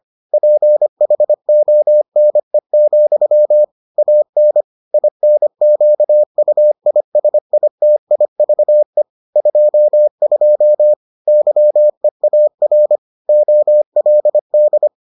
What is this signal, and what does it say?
Morse code, heard as a keyed tone, about 25 wpm, 600 Hz. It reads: PHONE, AN INQUISITIVE 22 YEAR OLD